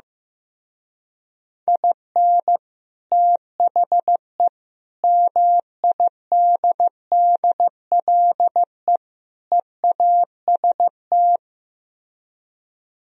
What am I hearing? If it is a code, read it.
Morse code, heard as a keyed tone, about 15 wpm, 700 Hz. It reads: IN THE MIDDLE EAST
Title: Speed-Racing - Sets of 4 Words 15wpm